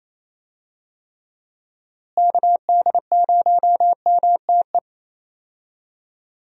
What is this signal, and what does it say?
Morse code, heard as a keyed tone, about 27 wpm, 700 Hz. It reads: KD0MTE
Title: USA Call Signs - 6 Characters - Repeated 2x - 28wpm